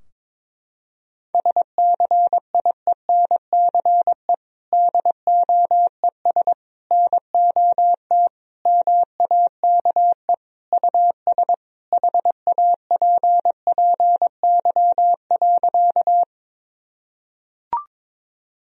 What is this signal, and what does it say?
Morse code, heard as a keyed tone, about 22 wpm, 700 Hz. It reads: SCIENCE DOES NOT MAKE US HAPPY.
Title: Sentences from Top 1000 Words - Review Part 2 - 22wpm